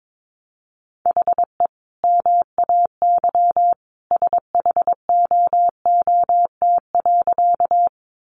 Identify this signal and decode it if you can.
Morse code, heard as a keyed tone, about 22 wpm, 700 Hz. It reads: HE MAY SHOOT.
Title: Sentences from Top 1200 Words 22wpm